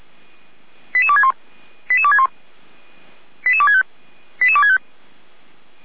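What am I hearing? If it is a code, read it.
Five-tone selective call (ZVEI), heard as quick runs of five tones, tones of about 70 ms. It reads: 88261 88261 88265 88265
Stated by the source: ZVEI five-tone call 88247 received by SDR, from the feuerwehr-orlamuende.org audio archive.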